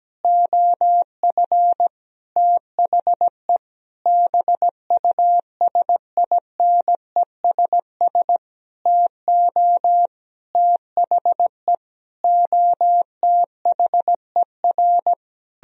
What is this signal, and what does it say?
Morse code, heard as a keyed tone, about 17 wpm, 700 Hz. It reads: OF THE BUSINESS TO THE OTHER
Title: Sentences from Top 1000 Words - Review Part 4 - 17wpm